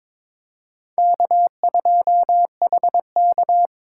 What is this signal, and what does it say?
Morse code, heard as a keyed tone, about 26 wpm, 700 Hz. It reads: K2HK
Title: Speed Racing - USA 1x2 Call Signs 22wpm